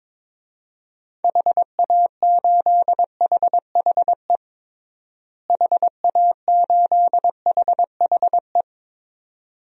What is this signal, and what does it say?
Morse code, heard as a keyed tone, about 22 wpm, 700 Hz. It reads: HA8HHE HA8HHE
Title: DXCC-100 Call Signs - 6 Characters - Repeated 2x - 22wpm